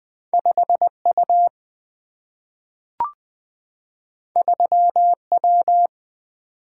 Morse code, heard as a keyed tone, about 20 wpm, 700 Hz. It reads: 5 U 3 W
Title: Single-Letter Single-Number 20wpm